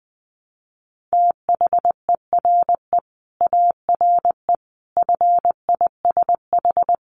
Morse code, heard as a keyed tone, about 20 wpm, 700 Hz. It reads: THERE ARE FISH
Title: Sentences from Top 1100 Words 20wpm